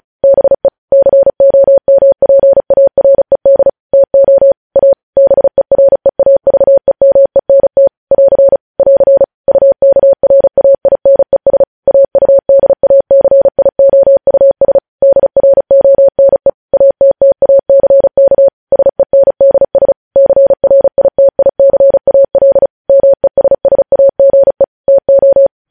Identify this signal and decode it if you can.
Morse code, heard as a keyed tone, about 36 wpm, 550 Hz. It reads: BE COMPARED TO A BEREAVEMENT <AR> <AR> UKRAINES AUDACIOUS DRONE ATTACK SENDS CRITICAL MESSAGE TO
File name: podcast / News headlines-1x-35x35